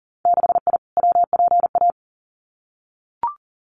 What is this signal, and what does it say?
Morse code, heard as a keyed tone, about 38 wpm, 700 Hz. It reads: 6I WPA
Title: Winter Field Day Exchanges Encore 40wpm